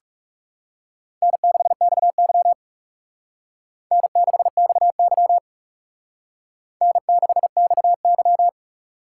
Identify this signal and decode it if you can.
Morse code, heard as a keyed tone, about 39 wpm, 700 Hz. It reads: N6XY N6XY N6XY